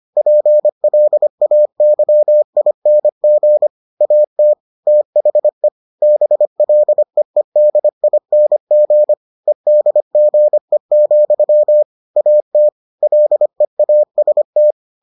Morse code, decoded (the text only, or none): PLAYING AT THE BLEEDING EDGE, AT LEAST